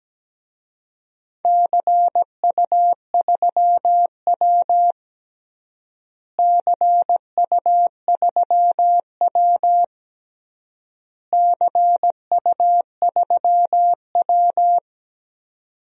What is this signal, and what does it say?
Morse code, heard as a keyed tone, about 17 wpm, 700 Hz. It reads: CU3W CU3W CU3W
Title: DXCC-100 Call Signs - 4 Characters - Repeated 3x - 17wpm